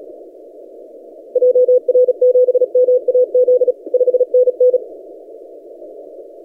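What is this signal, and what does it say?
Morse code, heard as a keyed tone, about 36 wpm, 500 Hz. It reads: JR7MAZ 5NN
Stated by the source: TX5K on 30 metres, CW.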